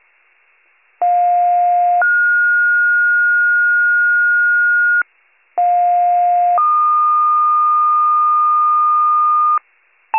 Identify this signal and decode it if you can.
Two-tone page, heard as a one-second tone, then a three-second tone, about 710 then 1430 Hz, about 710 then 1220 Hz, with noise. It is heard from 1.0 to 9.6 s.